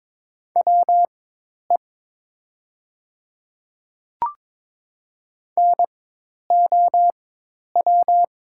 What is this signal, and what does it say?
Morse code, heard as a keyed tone, about 22 wpm, spaced as for 10 wpm, 700 Hz. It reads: WE NOW
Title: 4x Character Spacing - Top 100 Words - 22wpm